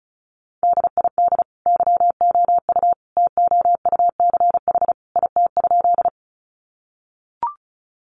Morse code, heard as a keyed tone, about 34 wpm, 700 Hz. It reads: DID YOU TOUCH IT?
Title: Sentences from Top 1200 Words 35wpm